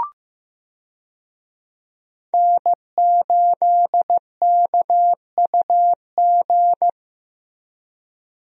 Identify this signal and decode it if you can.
Morse code, heard as a keyed tone, about 15 wpm, 700 Hz. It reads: N8KUG
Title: USA Call Signs - 5 Characters - Repeated 3x - 15wpm